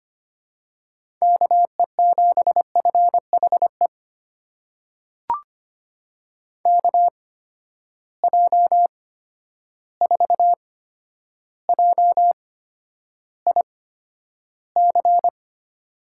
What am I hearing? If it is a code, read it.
Morse code, heard as a keyed tone, about 25 wpm, spaced as for 11 wpm, 700 Hz. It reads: KE7FHE KJ4JIC
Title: USA Call Signs - 6 Characters - 8x Character Spacing - 25wpm